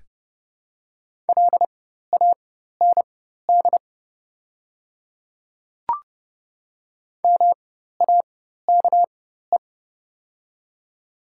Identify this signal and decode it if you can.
Morse code, heard as a keyed tone, about 30 wpm, spaced as for 14 wpm, 700 Hz. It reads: LAND MAKE
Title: 4x Character Spacing - Top 300 Words - 30wpm